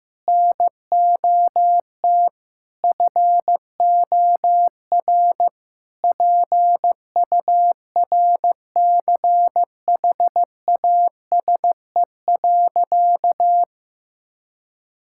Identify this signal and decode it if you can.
Morse code, heard as a keyed tone, about 15 wpm, 700 Hz. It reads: NOT FOR PURCHASE.
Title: Sentences from Top 1000 Words - Review Part 5 - 15wpm